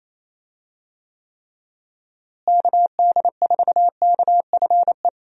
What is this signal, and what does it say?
Morse code, heard as a keyed tone, about 28 wpm, 700 Hz. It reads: KD4KFE
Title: USA Call Signs - 6 Characters - Repeated 3x - 28wpm